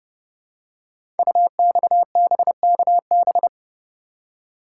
Morse code, heard as a keyed tone, about 29 wpm, 700 Hz. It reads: UXBKB
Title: Code Groups - 5 Letters - 30wpm